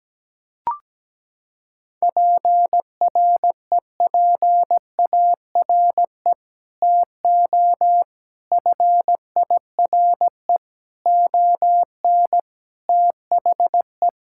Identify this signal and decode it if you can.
Morse code, heard as a keyed tone, about 17 wpm, 700 Hz. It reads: PREPARE TO FIRE ON THE
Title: Sentences from Top 1000 Words - Review Part 6 - 17wpm